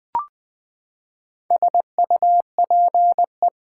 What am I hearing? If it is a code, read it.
Morse code, heard as a keyed tone, about 20 wpm, 700 Hz. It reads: SUPE
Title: Build A Word - 4 Letters At A Time - 10-Letter Words - 20wpm